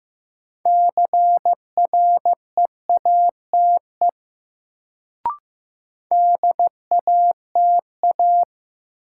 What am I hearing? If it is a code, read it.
Morse code, heard as a keyed tone, about 15 wpm, 700 Hz. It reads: CREATE DATA
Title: Sentences from Top 1000 Words - Review Part 3 - New Word Drill - 15wpm